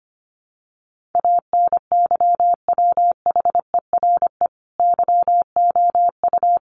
Morse code, heard as a keyed tone, about 25 wpm, 700 Hz. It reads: ANYWHERE YOU
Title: Sentences from Top 1400 Words 25wpm